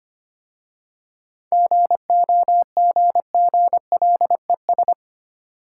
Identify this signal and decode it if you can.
Morse code, heard as a keyed tone, about 25 wpm, 700 Hz. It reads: GOGGLES
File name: practice / 25wpm / Easy Words with Hints - 25wpm